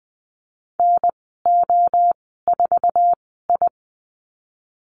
Morse code, heard as a keyed tone, about 20 wpm, spaced as for 15 wpm, 700 Hz. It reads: NO4I
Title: 055 Lesson 10g - Letter r - Call Signs 20wpm